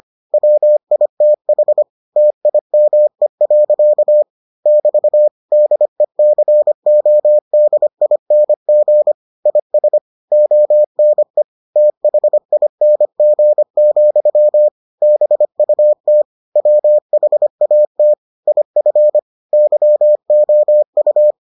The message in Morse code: WITH TIME. <BT> DECODING IS ONE THING, BUT WHAT IF YOU

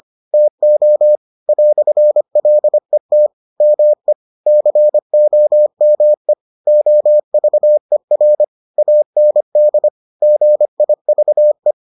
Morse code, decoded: TO "LET ME COME OVER AND GIVE

